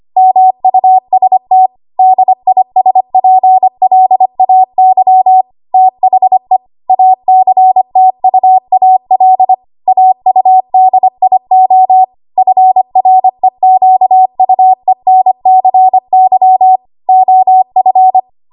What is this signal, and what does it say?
Morse code, heard as a keyed tone, about 25 wpm, 750 Hz. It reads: MUST DISPLAY THE ACTUAL AUDIO FREQUENCY OF
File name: Archive / 25 WPM / 25WPM